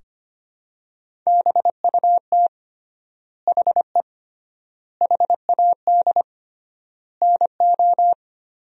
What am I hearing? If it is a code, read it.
Morse code, heard as a keyed tone, about 25 wpm, 700 Hz. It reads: BUT HE HAD NO
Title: 3x Word-Spacing - Sentences from Top 100 Words - 25wpm